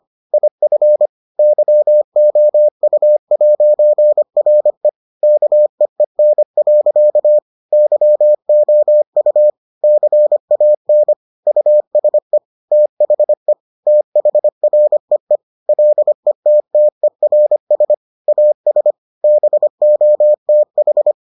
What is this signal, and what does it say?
Morse code, heard as a keyed tone, about 25 wpm, 600 Hz. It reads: IF YOU'RE KEEN. YOU CAN USE THE THREE LETTERS AS BOTH